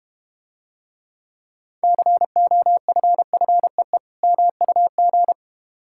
Morse code, heard as a keyed tone, about 31 wpm, 700 Hz. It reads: COFFEE MUG